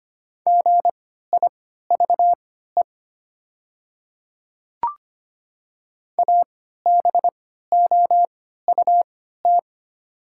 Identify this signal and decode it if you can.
Morse code, heard as a keyed tone, about 25 wpm, spaced as for 14 wpm, 690 Hz. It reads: GIVE ABOUT
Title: 3x Character Spacing - Top 300 Words - 25wpm